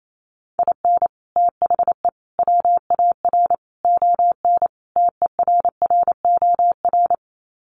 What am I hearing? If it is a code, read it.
Morse code, heard as a keyed tone, about 28 wpm, 700 Hz. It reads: IN THE WAR ON TERROR
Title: Sets of 5 Words Encore2 28wpm